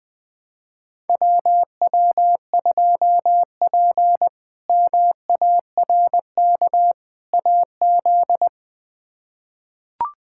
Morse code, heard as a keyed tone, about 21 wpm, 700 Hz. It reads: WW2P MARK AZ
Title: K1USN SST QSO Practice - 20wpm